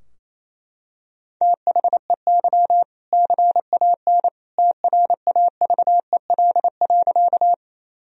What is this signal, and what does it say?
Morse code, heard as a keyed tone, about 28 wpm, 700 Hz. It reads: THEY CAN TRAVEL.
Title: Sentences from Top 1000 Words - Review Part 2 - 28wpm